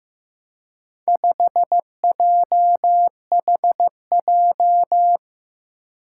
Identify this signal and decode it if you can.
Morse code, heard as a keyed tone, about 15 wpm, 700 Hz. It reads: 5JHJ
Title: Four Challenging Letter-Number Combinations 15wpm